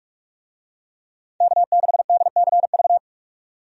Morse code, heard as a keyed tone, about 46 wpm, 700 Hz. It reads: K6DCV
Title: USA Call Signs - 5 Characters - 45wpm